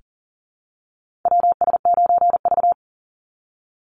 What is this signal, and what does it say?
Morse code, heard as a keyed tone, about 40 wpm, 700 Hz. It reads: WS9V
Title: Call Signs Repeated 2x - 40wpm